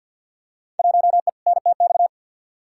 Morse code, heard as a keyed tone, about 53 wpm, 700 Hz. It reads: 1E NTX
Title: Field Day Exchanges - 50wpm